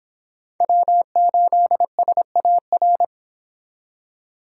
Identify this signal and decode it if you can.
Morse code, heard as a keyed tone, about 26 wpm, 700 Hz. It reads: W8SAR